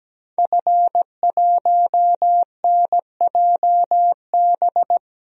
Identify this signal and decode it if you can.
Morse code, heard as a keyed tone, about 17 wpm, 700 Hz. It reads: F1NJB
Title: DXCC-100 Call Signs - 5 Characters - Repeated 2x - 17wpm